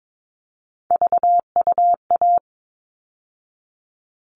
Morse code, heard as a keyed tone, about 22 wpm, 700 Hz. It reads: VUA
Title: All Cut Numbers - 3-Digit 22wpm